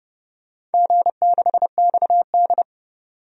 Morse code, heard as a keyed tone, about 30 wpm, 700 Hz. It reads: G6XD
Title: DXCC-100 Call Signs - 4 Characters - 30wpm